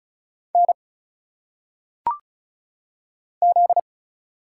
Morse code, heard as a keyed tone, about 34 wpm, 700 Hz. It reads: N Z
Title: Single Letters - Phonetic Pronunciation 35wpm